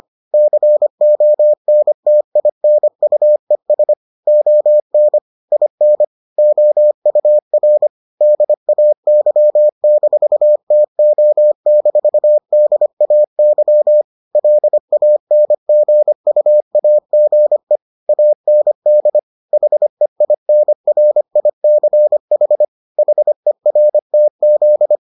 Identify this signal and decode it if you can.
Morse code, heard as a keyed tone, about 25 wpm, 600 Hz. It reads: CONTINUES ON IN OUR DAY-TO-DAY LANGUAGE AND HEINRICH HERTZ